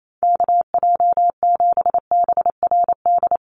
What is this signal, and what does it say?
Morse code, heard as a keyed tone, about 28 wpm, 700 Hz. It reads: KJ7BRD
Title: USA Call Signs 28wpm